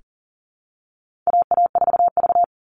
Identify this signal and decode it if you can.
Morse code, heard as a keyed tone, about 40 wpm, 700 Hz. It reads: AA4V
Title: Call Signs Repeated 3x - 40wpm